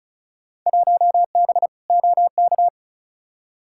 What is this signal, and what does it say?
Morse code, heard as a keyed tone, about 36 wpm, 700 Hz. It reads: 1B OK
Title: Field Day Exchanges Encore - 35wpm